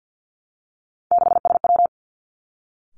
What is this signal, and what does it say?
Morse code, heard as a keyed tone, about 50 wpm, 700 Hz. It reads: 6SW
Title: Three Challenging Letter-Number Combinations 50wpm